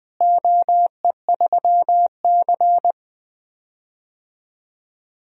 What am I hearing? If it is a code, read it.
Morse code, heard as a keyed tone, about 20 wpm, 700 Hz. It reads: OE3C
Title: DXCC-100 Call Signs - 4 Characters - Repeated 3x - 20wpm